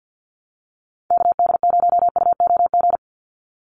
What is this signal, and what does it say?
Morse code, heard as a keyed tone, about 50 wpm, 700 Hz. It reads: KD0UOG
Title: USA Call Signs 50wpm